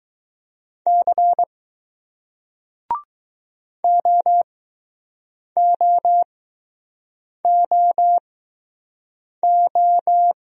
Morse code, heard as a keyed tone, about 20 wpm, spaced as for 6 wpm, 700 Hz. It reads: C OOOO